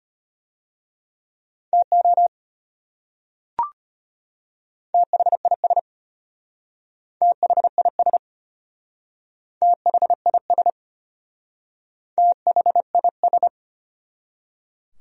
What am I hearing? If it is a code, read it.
Morse code, heard as a keyed tone, about 32 wpm, 700 Hz. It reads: TO THIS THIS THIS THIS